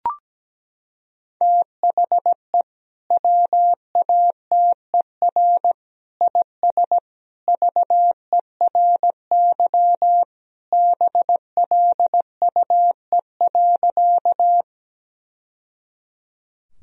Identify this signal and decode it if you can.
Morse code, heard as a keyed tone, about 17 wpm, 700 Hz. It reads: THE WATER IS VERY BLUE.
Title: Sentences from Top 1000 Words - Review Part 3 - 17wpm